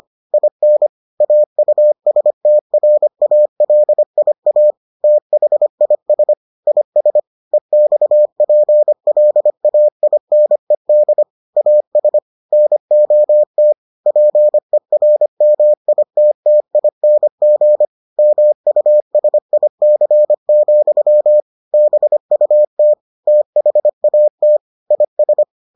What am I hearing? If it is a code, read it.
Morse code, heard as a keyed tone, about 25 wpm, 600 Hz. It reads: IN AUSTRALIA THIS IS EXPLAINED AS NOT PERMITTING MUSIC, BUT THAT IS